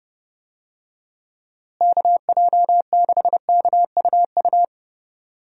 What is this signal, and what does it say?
Morse code, heard as a keyed tone, about 30 wpm, 700 Hz. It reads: KJ6KUU
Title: USA Call Signs - 6 Characters - Repeated 3x - 30wpm